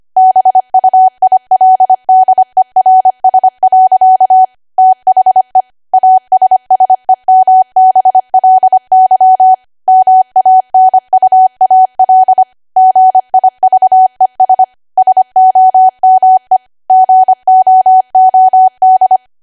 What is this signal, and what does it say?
Morse code, heard as a keyed tone, about 25 wpm, 750 Hz. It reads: BUILDERS. THE ASSEMBLY MANUAL GIVES SOME GOOD